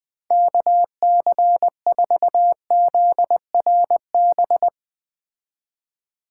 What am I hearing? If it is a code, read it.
Morse code, heard as a keyed tone, about 20 wpm, 700 Hz. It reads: KC4ZRB